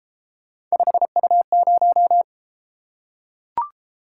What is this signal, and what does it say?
Morse code, heard as a keyed tone, about 33 wpm, 700 Hz. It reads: 5U0